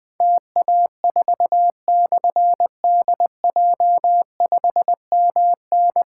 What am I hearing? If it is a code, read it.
Morse code, heard as a keyed tone, about 20 wpm, 700 Hz. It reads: TA4/DJ5MN
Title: Difficult Call Signs 20wpm